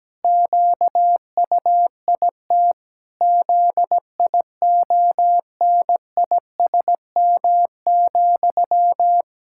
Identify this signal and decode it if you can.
Morse code, heard as a keyed tone, about 17 wpm, 700 Hz. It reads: QUIT ZIONISM,
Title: Pangram Challenge v2 - 17wpm